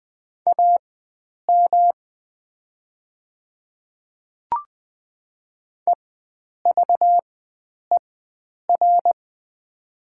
Morse code, heard as a keyed tone, about 20 wpm, spaced as for 9 wpm, 700 Hz. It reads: AM EVER
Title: 4x Character Spacing - Top 500 Words - 20wpm